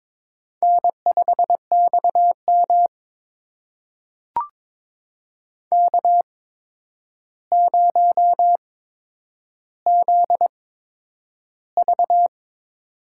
Morse code, heard as a keyed tone, about 22 wpm, spaced as for 9 wpm, 700 Hz. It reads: N5XM K0ZV
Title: USA Call Signs - 4 Characters - 8x Character Spacing - 22wpm